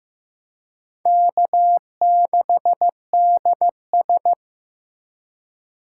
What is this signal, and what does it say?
Morse code, heard as a keyed tone, about 15 wpm, 700 Hz. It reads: K6DS
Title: USA Call Signs - 4 Characters - 3x Character Spacing - 15wpm